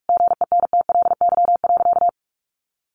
Morse code, heard as a keyed tone, about 44 wpm, 700 Hz. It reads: GENTLY.
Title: Sentences from Top 1500 Words 45wpm